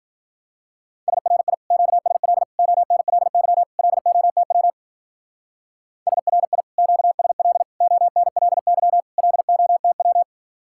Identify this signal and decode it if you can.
Morse code, heard as a keyed tone, about 47 wpm, 700 Hz. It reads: SRI QSL ONLY LOTW SRI QSL ONLY LOTW
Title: Speed-Racing Common QSO Phrases - 40wpm